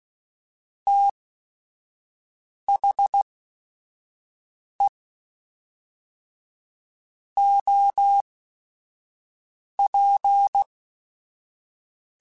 Morse code, heard as a keyed tone, about 16 wpm, spaced as for 5 wpm, 800 Hz. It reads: THE OP
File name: Technical Subjects / Technical Subjects-mp3s / XL @5